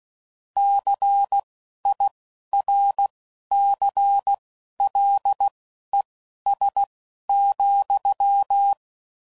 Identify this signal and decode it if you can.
Morse code, heard as a keyed tone, about 16 wpm, spaced as for 11 wpm, 800 Hz. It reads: CIRCLES,
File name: CW Code Groups @10WPM / Technical Subjects / XL